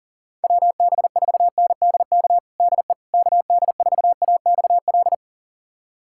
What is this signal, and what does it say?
Morse code, heard as a keyed tone, about 40 wpm, 700 Hz. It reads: WB4NDK DE KD4AXL
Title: USA Callsign DE Callsign - 6 Characters - 40wpm